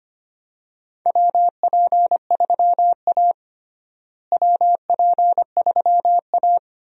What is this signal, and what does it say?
Morse code, heard as a keyed tone, about 25 wpm, 700 Hz. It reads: WP3A WP3A
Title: USA Call Signs - 4 Characters - Repeated 2x - 25wpm